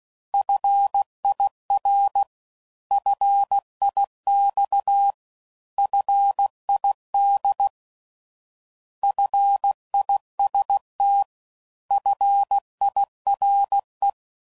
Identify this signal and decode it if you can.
Morse code, heard as a keyed tone, about 16 wpm, 800 Hz. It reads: FIR FIX FID FIST FIRE
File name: Prefixes / By First Letter / Fs